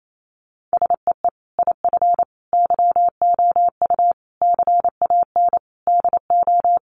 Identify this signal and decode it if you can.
Morse code, heard as a keyed tone, about 28 wpm, 700 Hz. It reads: SEE IF YOU CAN DO
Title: Sentences from Top 1000 Words 28wpm